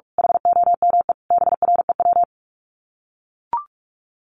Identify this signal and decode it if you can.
Morse code, heard as a keyed tone, about 44 wpm, 700 Hz. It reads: HOME BREW
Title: Common QSO Phrases 45wpm